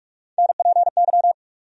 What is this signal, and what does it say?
Morse code, heard as a keyed tone, about 42 wpm, 700 Hz. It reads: NPY
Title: USA Call Sign Postfixes - 3 Letters 45wpm